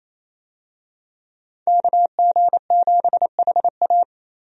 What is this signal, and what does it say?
Morse code, heard as a keyed tone, about 28 wpm, 700 Hz. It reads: KG7HA
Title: USA Call Signs - 5 Characters - 28wpm